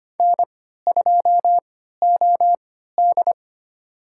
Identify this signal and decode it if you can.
Morse code, heard as a keyed tone, about 25 wpm, spaced as for 14 wpm, 700 Hz. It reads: N2OD
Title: USA Call Signs - 4 Characters - 3x Character Spacing - 25wpm